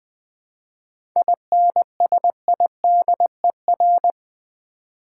Morse code, heard as a keyed tone, about 20 wpm, 700 Hz. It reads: INSIDER
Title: 7 Letter Words - First 3 Letters Given - 20wpm